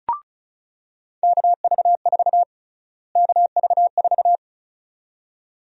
Morse code, heard as a keyed tone, about 34 wpm, 700 Hz. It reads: KV4 KV4
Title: USA Call Sign Prefixes - Repeated 2x - 35wpm